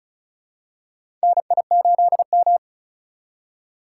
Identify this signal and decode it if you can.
Morse code, heard as a keyed tone, about 34 wpm, 700 Hz. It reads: NI8M
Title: USA Call Signs - 4 Characters - 2x Character Spacing - 35wpm